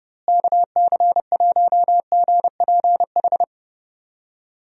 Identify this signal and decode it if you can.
Morse code, heard as a keyed tone, about 30 wpm, 700 Hz. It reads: KC1GPH
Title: USA Call Signs - 6 Characters - Repeated 3x - 30wpm